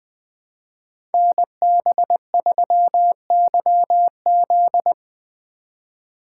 Morse code, heard as a keyed tone, about 20 wpm, 700 Hz. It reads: NB3YZ
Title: Code Groups - 5 Letter:Numbers - 20wpm